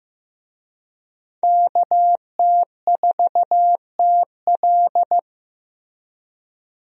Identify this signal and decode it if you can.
Morse code, heard as a keyed tone, about 15 wpm, 700 Hz. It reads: KT4TL
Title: USA Call Signs - 5 Characters - Repeated 2x - 15wpm